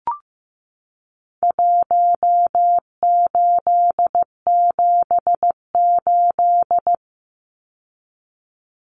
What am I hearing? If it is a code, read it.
Morse code, heard as a keyed tone, about 15 wpm, 700 Hz. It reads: 1878
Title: Numbers - Contest Style - 4-Digit 15wpm